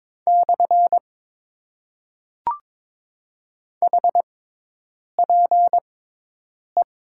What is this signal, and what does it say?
Morse code, heard as a keyed tone, about 22 wpm, 700 Hz. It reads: / H P E
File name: Top 90 QSO Elements - 6x Character Spacing - 22wpm